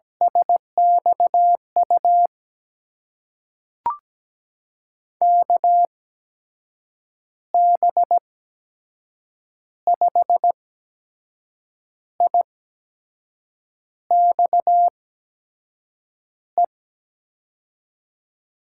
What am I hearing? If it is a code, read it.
Morse code, heard as a keyed tone, about 17 wpm, spaced as for 6 wpm, 700 Hz. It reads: SXU KB5IXE
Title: USA Call Signs - 6 Characters - 8x Character Spacing - 17wpm